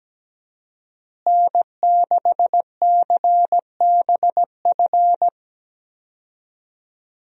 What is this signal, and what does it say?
Morse code, heard as a keyed tone, about 17 wpm, 700 Hz. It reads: N6CBF
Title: USA Call Signs - 5 Characters - Repeated 2x - 17wpm